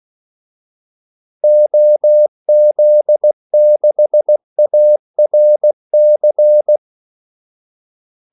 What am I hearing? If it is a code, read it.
Morse code, heard as a keyed tone, about 16 wpm, 600 Hz. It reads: OZ6ARC
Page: Kaldesignaler 16 wpm